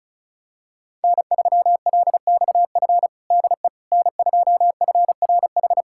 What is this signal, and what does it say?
Morse code, heard as a keyed tone, about 36 wpm, 700 Hz. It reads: N3LXF DE N2FRH
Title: USA Callsign DE Callsign - 5 Characters - 35wpm